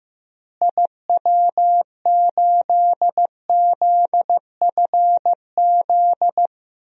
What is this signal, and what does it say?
Morse code, heard as a keyed tone, about 15 wpm, 700 Hz. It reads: IW8ZFZ
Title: DXCC-100 Call Signs - 6 Characters - Repeated 3x - 15wpm